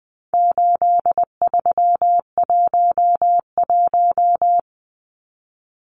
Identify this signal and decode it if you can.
Morse code, heard as a keyed tone, about 19 wpm, 700 Hz. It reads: 8311
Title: Numbers - 4-Digit 20wpm